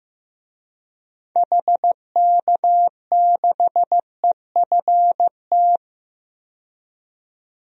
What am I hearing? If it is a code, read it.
Morse code, heard as a keyed tone, about 15 wpm, 700 Hz. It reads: HK6EFT
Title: DXCC-100 Call Signs - 6 Characters - 15wpm